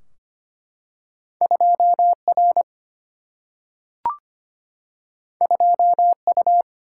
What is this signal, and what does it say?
Morse code, heard as a keyed tone, about 25 wpm, 700 Hz. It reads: 2R 2U
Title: Single-Letter Single-Number 25wpm